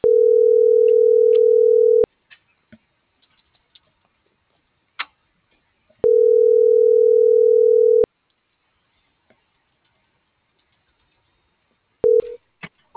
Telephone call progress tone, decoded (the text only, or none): ringback tone